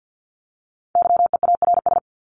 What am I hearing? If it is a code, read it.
Morse code, heard as a keyed tone, about 50 wpm, 700 Hz. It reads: YEARS
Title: ES:S Suffix - Top 100 Words with Base Words 50wpm